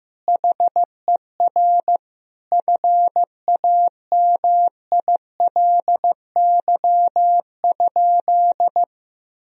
Morse code, heard as a keyed tone, about 16 wpm, 700 Hz. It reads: HER FAMILY?
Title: Sentences from Top 1000 Words - Review Part 4 - 15wpm